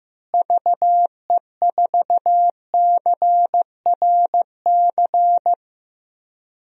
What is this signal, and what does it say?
Morse code, heard as a keyed tone, about 15 wpm, 700 Hz. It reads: VE4CRC
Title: DXCC-100 Call Signs - 6 Characters - Repeated 2x - 15wpm